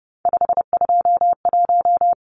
Morse code, heard as a keyed tone, about 30 wpm, 700 Hz. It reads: 521
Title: Numbers - 3-Digit 30wpm